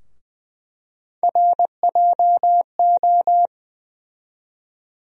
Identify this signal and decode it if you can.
Morse code, heard as a keyed tone, about 20 wpm, 700 Hz. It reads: RJO
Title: USA Call Sign Postfixes - 3 Letters 20wpm